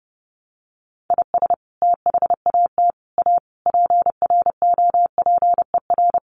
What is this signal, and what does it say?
Morse code, heard as a keyed tone, about 30 wpm, 700 Hz. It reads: IS THAT A PROPER
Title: Sentences from Top 1100 Words 30wpm